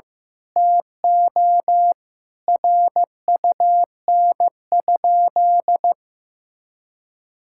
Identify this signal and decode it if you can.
Morse code, heard as a keyed tone, about 16 wpm, 700 Hz. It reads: TO RUN?
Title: Sentences from Top 1000 Words - Review Part 6 - 15wpm